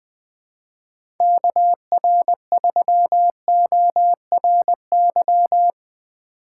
Morse code, heard as a keyed tone, about 21 wpm, 700 Hz. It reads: KR3ORY
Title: USA Call Signs - 6 Characters - 2x Character Spacing - 20wpm